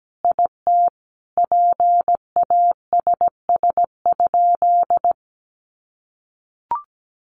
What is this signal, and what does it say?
Morse code, heard as a keyed tone, about 17 wpm, 700 Hz. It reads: IT PASS?
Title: Sentences from Top 400 Words 17wpm